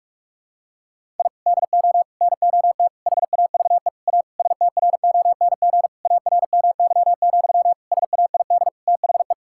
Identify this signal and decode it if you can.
Morse code, heard as a keyed tone, about 47 wpm, 700 Hz. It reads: I DO NOT HAVE A STRONG ARMY, SAID THE
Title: Sentences from Top 1000 Words - Review Part 5 - 45wpm